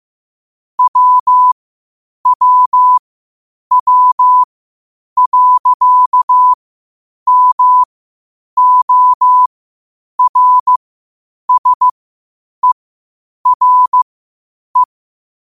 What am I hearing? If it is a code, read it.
Morse code, heard as a keyed tone, about 15 wpm, spaced as for 8 wpm, 1000 Hz. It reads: WWW.MORSERE
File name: qotdfiles / 8wpm